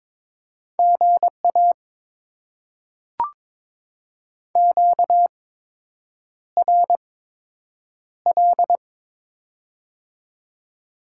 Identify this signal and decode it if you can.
Morse code, heard as a keyed tone, about 22 wpm, spaced as for 8 wpm, 700 Hz. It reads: GA QRL
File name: Top 20 QSO Elements - 8x Character Spacing - 22wpm